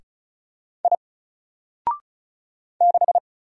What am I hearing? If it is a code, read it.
Morse code, heard as a keyed tone, about 36 wpm, 700 Hz. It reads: I 6
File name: Single-Letter-Number-35wpm